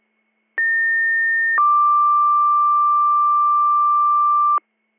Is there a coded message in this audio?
Yes: two-tone page from 0.6 to 4.6 s, heard as a one-second tone, then a three-second tone, about 1800 then 1180 Hz.